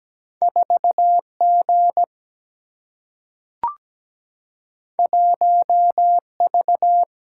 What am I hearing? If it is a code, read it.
Morse code, heard as a keyed tone, about 17 wpm, 700 Hz. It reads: 4 G 1 V
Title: Single-Letter Single-Number 17wpm